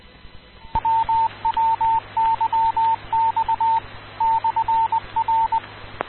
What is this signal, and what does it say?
Morse code, heard as a keyed tone, about 20 wpm, 900 Hz. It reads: WWYX/R
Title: Rocky Mountain Radio League Repeaters